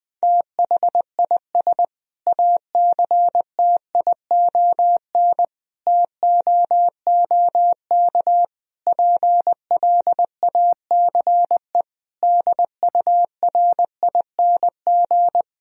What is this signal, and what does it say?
Morse code, heard as a keyed tone, about 20 wpm, 700 Hz. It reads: THIS ACTION TOOK PLACE DURING
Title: Sentences from Top 1000 Words - Review Part 4 - 20wpm